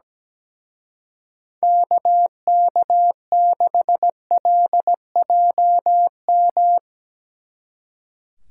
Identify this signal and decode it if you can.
Morse code, heard as a keyed tone, about 18 wpm, 700 Hz. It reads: KK6LJM
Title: Speed Racing - USA 2x3 Call Signs 17wpm